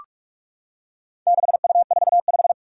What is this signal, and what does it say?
Morse code, heard as a keyed tone, about 41 wpm, 700 Hz. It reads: 6U45
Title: Four Challenging Letter-Number Combinations 45wpm